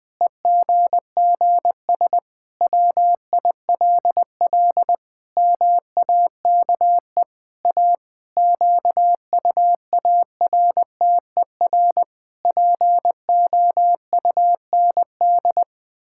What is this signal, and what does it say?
Morse code, heard as a keyed tone, about 20 wpm, 700 Hz. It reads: EGGS WILL MAKE A QUARTER POUND
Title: Pangram Challenge v2 - 20wpm